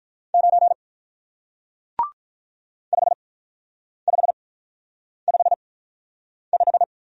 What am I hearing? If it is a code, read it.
Morse code, heard as a keyed tone, about 48 wpm, spaced as for 9 wpm, 700 Hz. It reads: 9 5555